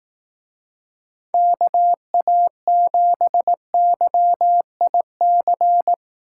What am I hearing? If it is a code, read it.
Morse code, heard as a keyed tone, about 18 wpm, 700 Hz. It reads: KA7YIC
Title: Speed Racing - USA 2x3 Call Signs 15wpm